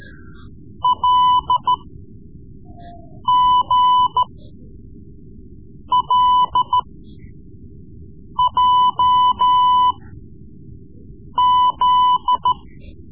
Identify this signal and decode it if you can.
Morse code, heard as a keyed tone, about 13 wpm, spaced as for 5 wpm, 1000 Hz. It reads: LGLJZ
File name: Morse Lessons / Morse Lessons / Tape-3B